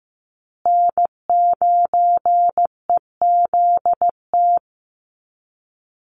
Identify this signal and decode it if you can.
Morse code, heard as a keyed tone, about 15 wpm, 700 Hz. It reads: N9EZT
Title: USA Call Signs Repeated 3x 15wpm